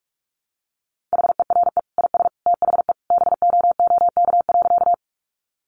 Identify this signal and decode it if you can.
Morse code, heard as a keyed tone, about 44 wpm, 700 Hz. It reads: HERE IS THE BOOK.
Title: Sentences from Top 300 Words 45wpm